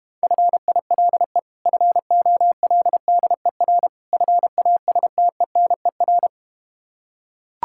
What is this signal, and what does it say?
Morse code, heard as a keyed tone, about 32 wpm, 700 Hz. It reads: FILE FOLDER FASTENER